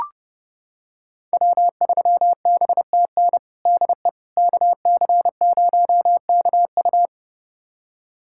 Morse code, heard as a keyed tone, about 30 wpm, 700 Hz. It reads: W3BTN DE KC0KU
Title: USA Callsign DE Callsign - 5 Characters - 30wpm